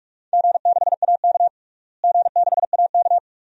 Morse code, heard as a keyed tone, about 44 wpm, 700 Hz. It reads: G6AK G6AK
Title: DXCC-100 Call Signs - 4 Characters - Repeated 2x - 45wpm